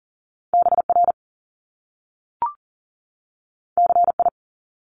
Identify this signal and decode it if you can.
Morse code, heard as a keyed tone, about 40 wpm, 700 Hz. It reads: BR CI
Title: Top Two Letter Combinations in Words 40wpm